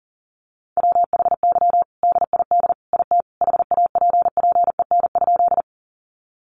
Morse code, heard as a keyed tone, about 40 wpm, 700 Hz. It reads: WHY DID IT HAPPEN?
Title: Sentences from Top 400 Words 40wpm